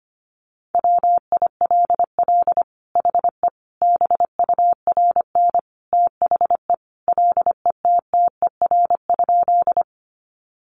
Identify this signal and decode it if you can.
Morse code, heard as a keyed tone, about 25 wpm, 700 Hz. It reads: WILL HE BURN THE LETTER?
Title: Sentences from Top 1300 Words 25wpm